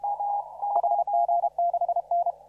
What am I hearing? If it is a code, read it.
Morse code, heard as a keyed tone, about 30 wpm, 690 Hz. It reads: HG6N
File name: CW / RHOB CW 2017 HA6P 0001